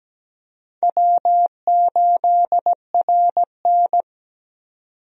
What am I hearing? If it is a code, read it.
Morse code, heard as a keyed tone, about 17 wpm, 700 Hz. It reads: W8RN
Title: USA Call Signs - 4 Characters - 17wpm